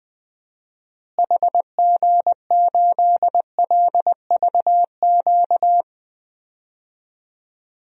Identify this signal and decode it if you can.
Morse code, heard as a keyed tone, about 20 wpm, 700 Hz. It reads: HG8LVQ
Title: DXCC-100 Call Signs - 6 Characters - Repeated 3x - 20wpm